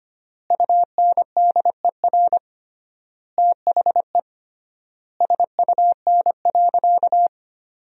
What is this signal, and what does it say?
Morse code, heard as a keyed tone, about 25 wpm, 700 Hz. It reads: UNDER THE SUN.
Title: 3x Word-Spacing - Sentences from Top 500 Words - 25wpm